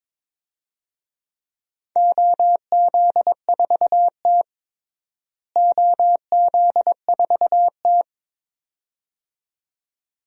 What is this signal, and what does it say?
Morse code, heard as a keyed tone, about 22 wpm, 700 Hz. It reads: OZ4T OZ4T
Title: DXCC-100 Call Signs - 4 Characters - Repeated 2x - 22wpm